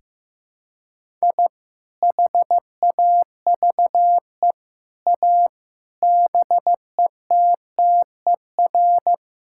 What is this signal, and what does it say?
Morse code, heard as a keyed tone, about 15 wpm, 700 Hz. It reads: I HAVE A BETTER
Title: Sentences from Top 1000 Words - Review Part 5 - 15wpm